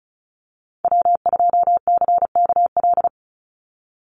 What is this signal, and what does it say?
Morse code, heard as a keyed tone, about 34 wpm, 700 Hz. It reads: W2CKL
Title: NA SOTA Activator Callsigns 35wpm